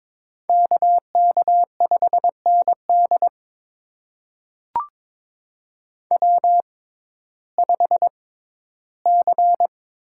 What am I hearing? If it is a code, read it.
Morse code, heard as a keyed tone, about 22 wpm, spaced as for 14 wpm, 700 Hz. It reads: KK5ND W5C
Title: USA Call Signs - 5 Characters - 6x Character Spacing - 22wpm